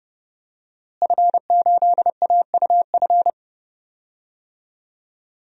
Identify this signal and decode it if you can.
Morse code, heard as a keyed tone, about 30 wpm, 700 Hz. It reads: F8AUF
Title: DXCC-100 Call Signs - 5 Characters - Repeated 2x - 30wpm